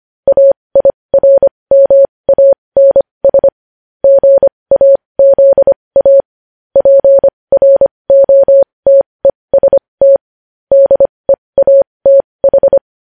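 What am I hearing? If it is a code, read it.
Morse code, heard as a keyed tone, about 25 wpm, spaced as for 20 wpm, 550 Hz. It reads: AIRMANS GAZA PROTEST DEATH
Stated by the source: News headlines sent in Morse code at 20.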